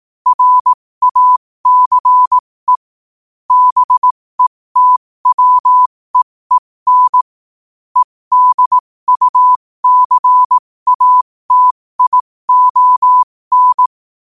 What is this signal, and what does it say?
Morse code, heard as a keyed tone, about 18 wpm, spaced as for 15 wpm, 1000 Hz. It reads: RACE BETWEEN EDUCATION